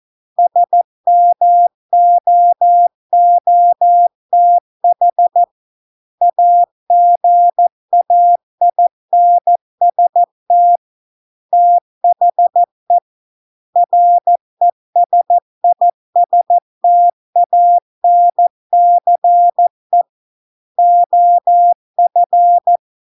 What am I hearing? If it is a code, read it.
Morse code, heard as a keyed tone, about 14 wpm, 700 Hz. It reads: SMOOTH AGAINST THE RESISTANCE OF